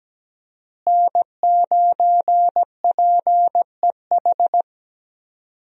Morse code, heard as a keyed tone, about 17 wpm, 700 Hz. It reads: N9PEH
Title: USA Call Signs - 5 Characters - Repeated 2x - 17wpm